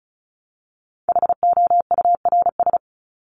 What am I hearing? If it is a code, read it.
Morse code, heard as a keyed tone, about 34 wpm, 700 Hz. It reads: HOURS
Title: ES:S Suffix - Top 100 Words with Base Words 35wpm